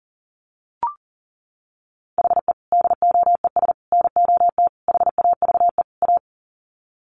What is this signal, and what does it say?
Morse code, heard as a keyed tone, about 40 wpm, 700 Hz. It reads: HE DOES NOT HAVE A
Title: Sets of 5 Words Encore 40wpm